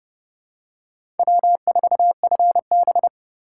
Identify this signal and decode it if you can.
Morse code, heard as a keyed tone, about 30 wpm, 700 Hz. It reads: W4FB
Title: USA Call Signs - 4 Characters - 8x Character Spacing - 30wpm